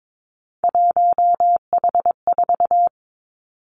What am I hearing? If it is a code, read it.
Morse code, heard as a keyed tone, about 22 wpm, 700 Hz. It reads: 1H4
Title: Three Challenging Letter-Number Combinations 22wpm